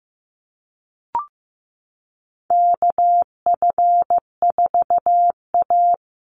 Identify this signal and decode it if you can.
Morse code, heard as a keyed tone, about 15 wpm, 700 Hz. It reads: KF4A
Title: Call Signs Repeated 2x - 15wpm